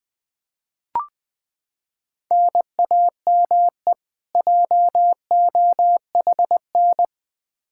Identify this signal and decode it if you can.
Morse code, heard as a keyed tone, about 21 wpm, 700 Hz. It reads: NAME JOHN
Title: Practice QSO - Exchange 1 - Beginning - 20wpm